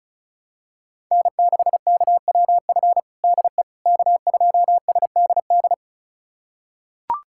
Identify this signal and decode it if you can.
Morse code, heard as a keyed tone, about 36 wpm, 700 Hz. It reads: N6KWF DE K2SDD
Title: USA Callsign DE Callsign - 5 Characters - 35wpm